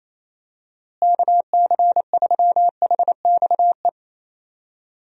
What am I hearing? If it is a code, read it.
Morse code, heard as a keyed tone, about 28 wpm, 700 Hz. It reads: KC3HXE
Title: USA Call Signs - 6 Characters - Repeated 3x - 28wpm